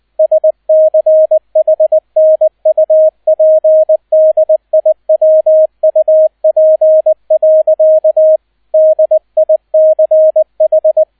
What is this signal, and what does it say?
Morse code, heard as a keyed tone, about 20 wpm, 630 Hz, with radio noise. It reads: SCHNUPDIWUP. DICH
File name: 002 Zweiter Streich 20wpm